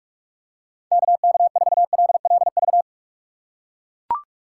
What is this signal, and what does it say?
Morse code, heard as a keyed tone, about 41 wpm, 700 Hz. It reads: KK4LLV
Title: USA Call Signs - 6x Character Spacing - 45wpm